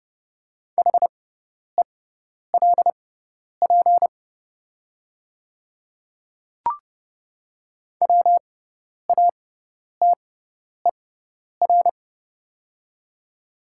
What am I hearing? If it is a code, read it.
Morse code, heard as a keyed tone, about 30 wpm, spaced as for 10 wpm, 700 Hz. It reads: HELP WATER
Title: 6x Character Spacing - Top 200 Words - 30wpm